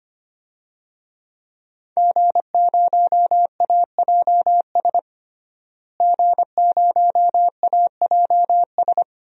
Morse code, heard as a keyed tone, about 25 wpm, 700 Hz. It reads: G0AJS G0AJS
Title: DXCC-100 Call Signs - 5 Characters - Repeated 3x - 25wpm